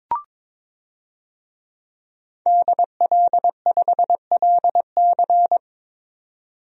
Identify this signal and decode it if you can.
Morse code, heard as a keyed tone, about 22 wpm, 700 Hz. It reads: DL5LC
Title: DXCC-100 Call Signs - 5 Characters - 22wpm